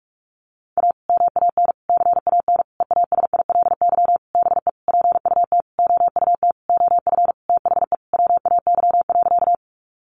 Morse code, heard as a keyed tone, about 44 wpm, 700 Hz. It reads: A MAN CAN EASILY BE PUT OUT OF THE WAY.
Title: Sentences from Top 900 Words 45wpm